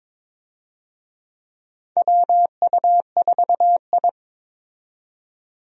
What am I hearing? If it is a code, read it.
Morse code, heard as a keyed tone, about 22 wpm, 700 Hz. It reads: WU4I
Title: USA Call Signs - 4 Characters - 22wpm